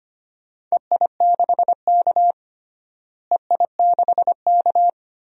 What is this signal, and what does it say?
Morse code, heard as a keyed tone, about 25 wpm, 700 Hz. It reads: EI6K EI6K
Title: DXCC-100 Call Signs - 4 Characters - Repeated 2x - 25wpm